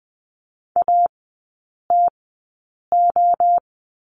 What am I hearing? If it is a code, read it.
Morse code, heard as a keyed tone, about 20 wpm, spaced as for 8 wpm, 700 Hz. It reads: ATO
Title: 053 Lesson 10e - Letter r - Three Characters 20wpm